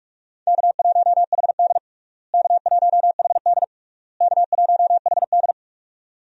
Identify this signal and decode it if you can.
Morse code, heard as a keyed tone, about 46 wpm, 700 Hz. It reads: K1HD K1HD K1HD
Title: USA Call Signs - 4 Characters - Repeated 3x - 45wpm